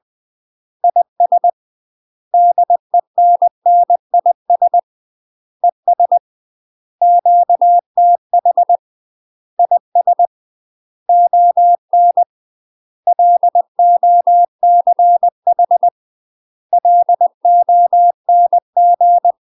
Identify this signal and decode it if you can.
Morse code, heard as a keyed tone, about 20 wpm, 700 Hz. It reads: IS DENNIS ES QTH IS ON LOCH LONG
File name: UK MASTER QSOs TRAINING / 20WPM / UK MASTER QSOs 0050